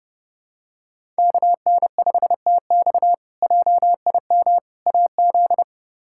Morse code, heard as a keyed tone, about 30 wpm, 700 Hz. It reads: KN5TX JIM AZ
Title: K1USN SST QSO Practice - 30wpm